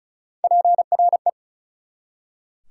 Morse code, heard as a keyed tone, about 34 wpm, 700 Hz. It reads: PRE
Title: Prefixes and Suffixes 35wpm